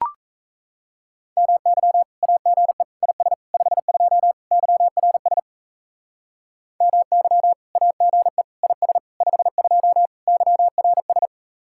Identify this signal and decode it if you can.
Morse code, heard as a keyed tone, about 35 wpm, 700 Hz. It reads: MY AGE IS 52 YRS MY AGE IS 52 YRS
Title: Speed-Racing Common QSO Phrases - 28wpm